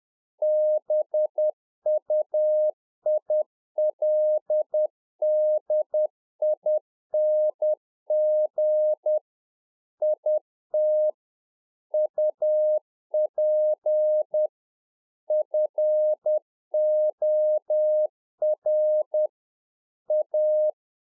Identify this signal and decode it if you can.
Morse code, heard as a keyed tone, about 10 wpm, 610 Hz. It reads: BUILDING IT UP FOR A
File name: 2011-04-mp3 / A SMALL TRANCEIVER 1 10WPM